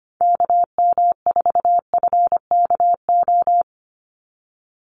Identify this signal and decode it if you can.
Morse code, heard as a keyed tone, about 25 wpm, 700 Hz. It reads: KM4FKO
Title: USA Call Signs 25wpm